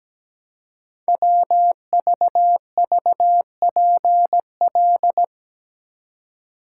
Morse code, heard as a keyed tone, about 17 wpm, 700 Hz. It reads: WVVPL
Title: Code Groups - 5 Letter:Numbers - 17wpm